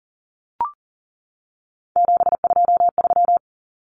Morse code, heard as a keyed tone, about 38 wpm, 700 Hz. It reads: 723
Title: Numbers - 3-Digit 40wpm